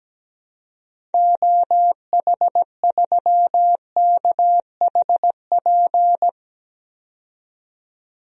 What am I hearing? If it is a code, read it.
Morse code, heard as a keyed tone, about 17 wpm, 700 Hz. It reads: OH3KHP
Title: DXCC-100 Call Signs - 6 Characters - 17wpm